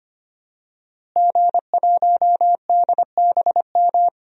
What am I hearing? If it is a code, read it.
Morse code, heard as a keyed tone, about 25 wpm, 700 Hz. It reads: G1DBM
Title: DXCC-100 Call Signs - 5 Characters - Repeated 3x - 25wpm